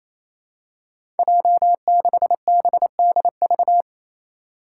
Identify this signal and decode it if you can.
Morse code, heard as a keyed tone, about 28 wpm, 700 Hz. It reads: J6BDV
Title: Five Challenging Letter-Number Combinations 28wpm